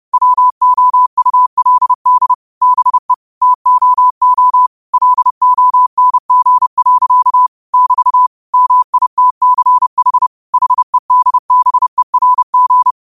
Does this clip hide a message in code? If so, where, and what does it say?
Yes: Morse code from 0.1 to 12.9 s, heard as a keyed tone, about 30 wpm, 1000 Hz. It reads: WOULD BE TOO LONG. <BT> MITCH HEDBERG